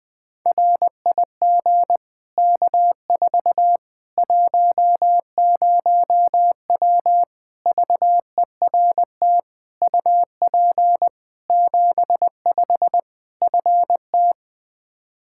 Morse code, heard as a keyed tone, about 21 wpm, 700 Hz. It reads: RIG K4 10W VERT UP 75 FT
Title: Practice QSO - Exchange 2 - Info - 20wpm